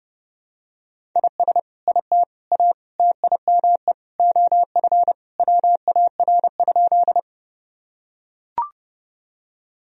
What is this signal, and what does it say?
Morse code, heard as a keyed tone, about 30 wpm, 700 Hz. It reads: IS IT A TIME OF WAR?
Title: Sentences from Top 1000 Words - Review Part 2 - 30wpm